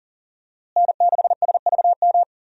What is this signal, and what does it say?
Morse code, heard as a keyed tone, about 39 wpm, 700 Hz. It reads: N6SVM
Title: USA Call Signs - 5 Characters - Repeated 3x - 40wpm